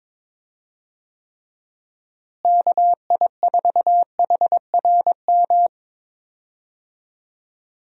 Morse code, heard as a keyed tone, about 22 wpm, 700 Hz. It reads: KI4HRM
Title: USA Call Signs - 6 Characters - Repeated 3x - 22wpm